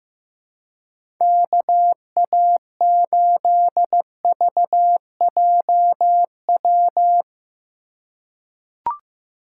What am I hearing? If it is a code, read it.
Morse code, heard as a keyed tone, about 15 wpm, 700 Hz. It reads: KA8VJW
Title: USA Call Signs - 3x Character Spacing - 15wpm